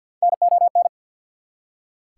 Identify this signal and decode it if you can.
Morse code, heard as a keyed tone, about 48 wpm, 690 Hz. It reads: NON
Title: Prefixes and Suffixes 50wpm